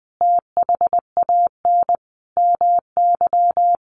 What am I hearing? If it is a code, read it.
Morse code, heard as a keyed tone, about 20 wpm, 700 Hz. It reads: THAN MY
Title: Sentences from Top 1300 Words 20wpm